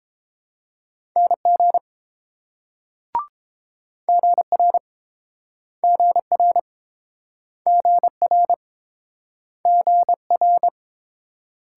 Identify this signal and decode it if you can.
Morse code, heard as a keyed tone, about 30 wpm, 700 Hz. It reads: NG GR GR GR GR